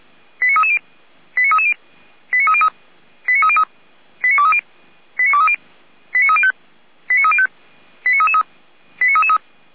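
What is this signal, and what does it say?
Five-tone selective call (ZVEI), heard as quick runs of five tones, tones of about 70 ms. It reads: 89330 89330 89303 89303 89230 89230 89305 89305 89333 89333
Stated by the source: ZVEI five-tone selective call received by SDR, from the feuerwehr-orlamuende.org fire brigade server.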